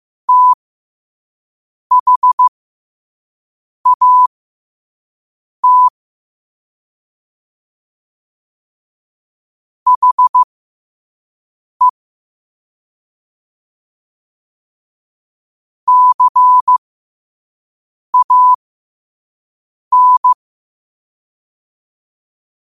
Morse code, heard as a keyed tone, about 15 wpm, spaced as for 5 wpm, 1000 Hz. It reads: THAT HE CAN
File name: 5wpm